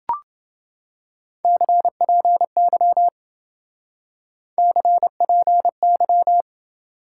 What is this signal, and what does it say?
Morse code, heard as a keyed tone, about 28 wpm, 700 Hz. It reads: CPY CPY